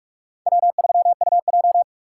Morse code, heard as a keyed tone, about 44 wpm, 700 Hz. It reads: W3UJ
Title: USA Call Signs - 4 Characters - 2x Character Spacing - 45wpm